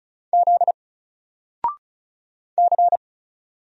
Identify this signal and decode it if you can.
Morse code, heard as a keyed tone, about 35 wpm, 700 Hz. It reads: Z C
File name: Single-Letter-Number-35wpm